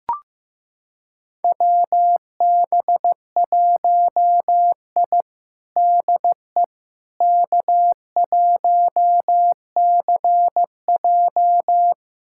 Morse code, heard as a keyed tone, about 16 wpm, 700 Hz. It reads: WB1I DE K1CJ
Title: USA Callsign DE Callsign - 4 Characters - 15wpm